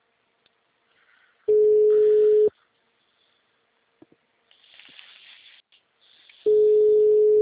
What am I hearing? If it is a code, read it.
Ringback tone (Europe, 425 Hz).